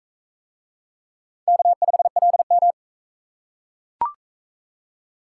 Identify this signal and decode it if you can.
Morse code, heard as a keyed tone, about 37 wpm, 700 Hz. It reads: K5LM